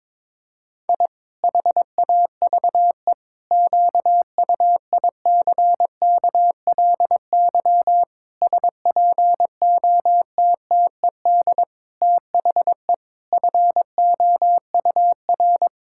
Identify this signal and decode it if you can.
Morse code, heard as a keyed tone, about 22 wpm, 700 Hz. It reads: I HAVE QUICKLY SPOTTED THE FOUR
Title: Pangram Challenge v2 - 22wpm